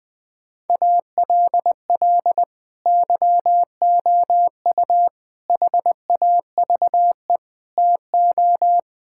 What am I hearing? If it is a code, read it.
Morse code, heard as a keyed tone, about 20 wpm, 700 Hz. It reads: ALL YOU HAVE TO
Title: Sentences with 8-Word Phrases - 20wpm